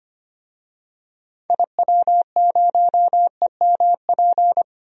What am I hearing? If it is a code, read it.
Morse code, heard as a keyed tone, about 25 wpm, 700 Hz. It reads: IW0EMP
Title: DXCC-100 Call Signs - 6 Characters - Repeated 3x - 25wpm